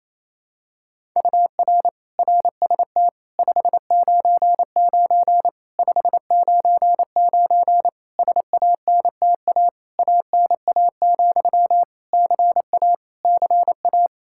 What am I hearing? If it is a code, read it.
Morse code, heard as a keyed tone, about 28 wpm, 700 Hz. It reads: UR RST 599 599 SANTA ANA, CA CA
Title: Practice QSO - Exchange 1 - Beginning - 28wpm